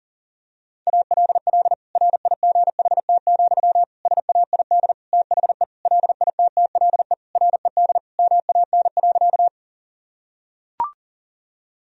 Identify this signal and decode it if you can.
Morse code, heard as a keyed tone, about 37 wpm, 700 Hz. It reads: ALL RIGHT, SAID THE LITTLE RED MAN.
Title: Sentences from Top 1000 Words - Review Part 2 - 40wpm